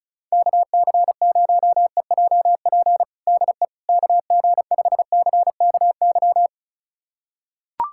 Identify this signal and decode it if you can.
Morse code, heard as a keyed tone, about 34 wpm, 700 Hz. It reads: KC0EJP DE KG5CKY
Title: USA Callsign DE Callsign - 6 Characters - 35wpm